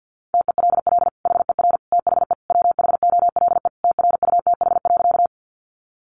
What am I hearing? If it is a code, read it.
Morse code, heard as a keyed tone, about 50 wpm, 700 Hz. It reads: TELL HER THE WHOLE TRUTH.
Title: Sentences from Top 800 Words 50wpm